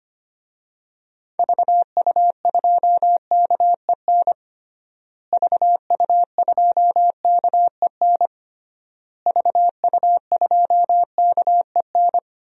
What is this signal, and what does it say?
Morse code, heard as a keyed tone, about 25 wpm, 700 Hz. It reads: VU2KEN VU2KEN VU2KEN
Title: DXCC-100 Call Signs - 6 Characters - Repeated 3x - 25wpm